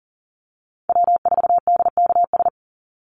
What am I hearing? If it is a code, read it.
Morse code, heard as a keyed tone, about 40 wpm, 700 Hz. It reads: W4DKS
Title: Call Signs - 40wpm